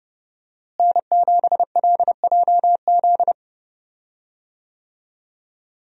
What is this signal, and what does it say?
Morse code, heard as a keyed tone, about 30 wpm, 700 Hz. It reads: N7LJZ
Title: USA Call Signs - 5 Characters - Repeated 2x - 30wpm